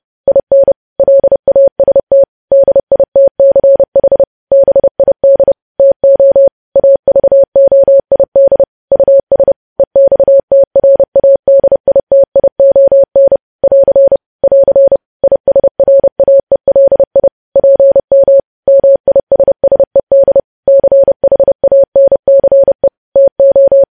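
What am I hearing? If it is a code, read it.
Morse code, heard as a keyed tone, about 30 wpm, 550 Hz. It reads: IN LAST DITCH BID TO AVOID US EXTRADITION <AR> <AR> ISRAELI PM MISSED CHANCE TO
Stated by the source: Morse code news headlines practice, preceded by vvv.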